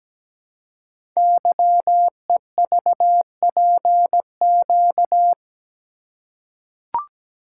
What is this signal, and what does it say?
Morse code, heard as a keyed tone, about 17 wpm, 700 Hz. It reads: YEVPQ
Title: Code Groups - 5 Letters - 17wpm